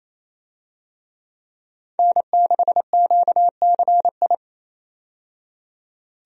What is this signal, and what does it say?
Morse code, heard as a keyed tone, about 28 wpm, 700 Hz. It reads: N6QCI
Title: USA Call Signs - 5 Characters - 28wpm